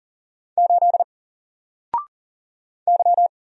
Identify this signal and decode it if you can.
Morse code, heard as a keyed tone, about 40 wpm, 700 Hz. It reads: 8 Y
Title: Single Character (Letter:Number:Punctuation:Prosign) Alt 40wpm